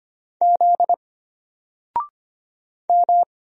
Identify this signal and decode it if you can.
Morse code, heard as a keyed tone, about 25 wpm, 700 Hz. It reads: Z M
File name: Single-Letter-Number-25wpm